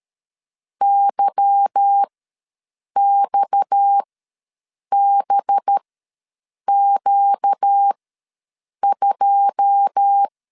Morse code, heard as a keyed tone, about 13 wpm, spaced as for 7 wpm, 800 Hz. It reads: YXBQ2
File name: LEZIONI / 10 SIGNS / 10A SLASH ws 12wpm 3443